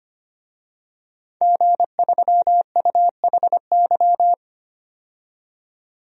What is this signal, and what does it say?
Morse code, heard as a keyed tone, about 25 wpm, 700 Hz. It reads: G3UHY
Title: DXCC-100 Call Signs - 5 Characters - 25wpm